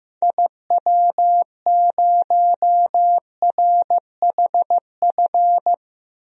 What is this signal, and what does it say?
Morse code, heard as a keyed tone, about 15 wpm, 700 Hz. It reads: IW0RHF
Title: DXCC-100 Call Signs - 6 Characters - Repeated 3x - 15wpm